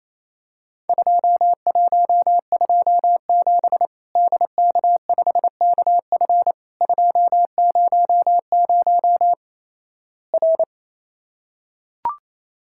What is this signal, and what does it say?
Morse code, heard as a keyed tone, about 28 wpm, 700 Hz. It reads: 2127 DK5KF 200 R
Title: Worked All Europe DX Contest - QTC Exchange - 28wpm